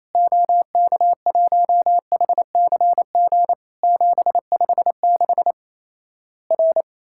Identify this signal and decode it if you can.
Morse code, heard as a keyed tone, about 28 wpm, 700 Hz. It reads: OK1HCG 756 R
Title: Worked All Europe DX Contest - QTC Exchange - 28wpm